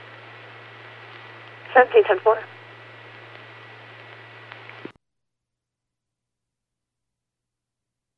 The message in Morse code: none

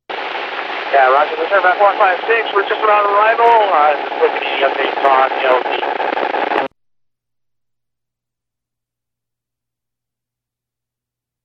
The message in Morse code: none

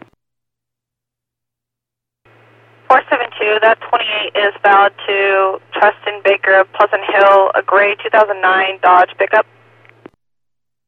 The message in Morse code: none

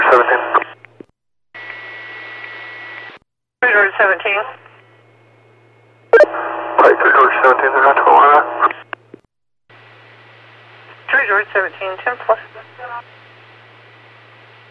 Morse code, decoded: none

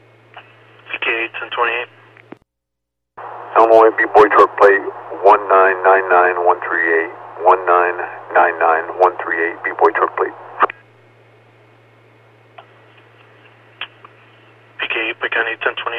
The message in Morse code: none

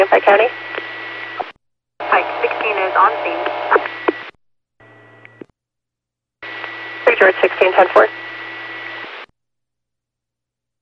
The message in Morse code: none

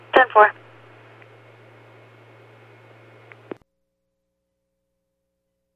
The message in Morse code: none

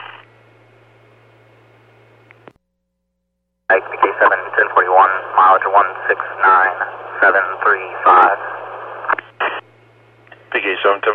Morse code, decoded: none

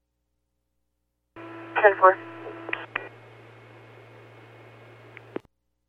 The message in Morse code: none

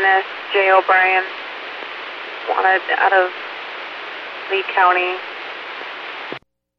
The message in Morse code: none

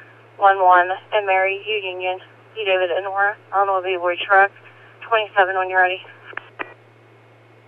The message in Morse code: none

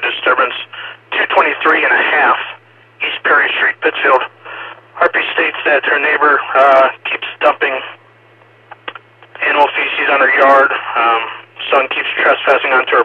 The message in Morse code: none

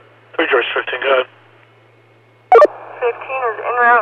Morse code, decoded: none

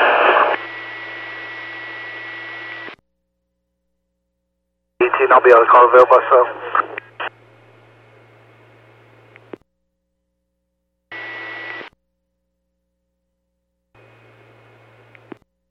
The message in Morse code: none

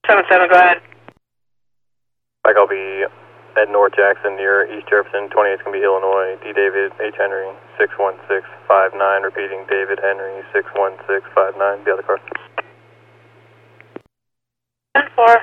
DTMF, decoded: none

